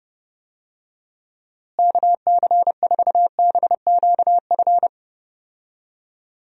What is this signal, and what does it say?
Morse code, heard as a keyed tone, about 29 wpm, 700 Hz. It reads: KC4BQF
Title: USA Call Signs - 6 Characters - Repeated 3x - 30wpm